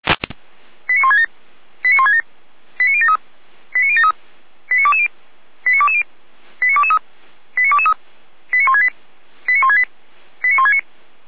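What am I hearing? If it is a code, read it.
Five-tone selective call (ZVEI), heard as quick runs of five tones, tones of about 70 ms. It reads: 89167 89167 89073 89073 89220 89220 89223 89223 89168 89168 89169
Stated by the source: ZVEI five-tone call received by SDR, from the server of Feuerwehr Orlamünde (fire brigade).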